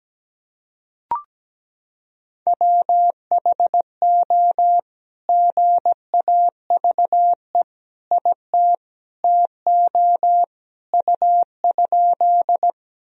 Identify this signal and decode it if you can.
Morse code, heard as a keyed tone, about 17 wpm, 700 Hz. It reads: WHO GAVE IT TO U?
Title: Sentences from Top 500 Words and Abbreviations 17wpm